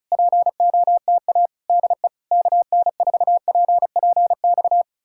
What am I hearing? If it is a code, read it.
Morse code, heard as a keyed tone, about 40 wpm, 700 Hz. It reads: POTA DE KN4PPX